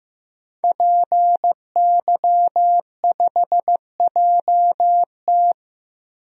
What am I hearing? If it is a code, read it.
Morse code, heard as a keyed tone, about 15 wpm, 700 Hz. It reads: PY5JT
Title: DXCC-100 Call Signs - 5 Characters - 15wpm